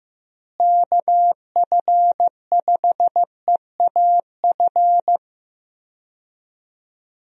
Morse code, heard as a keyed tone, about 15 wpm, 700 Hz. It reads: KF5EAF
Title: USA Call Signs - 6 Characters - Repeated 2x - 15wpm